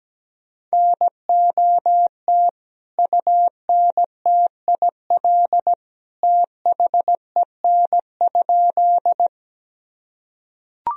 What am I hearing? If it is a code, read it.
Morse code, heard as a keyed tone, about 17 wpm, 700 Hz. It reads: NOT UNTIL THEN?
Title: Sentences from Top 1000 Words - Review Part 1 - 17wpm